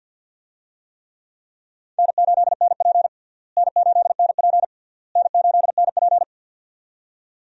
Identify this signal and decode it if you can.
Morse code, heard as a keyed tone, about 52 wpm, 700 Hz. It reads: N8NP N8NP N8NP
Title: USA Call Signs - 4 Characters - Repeated 3x - 50wpm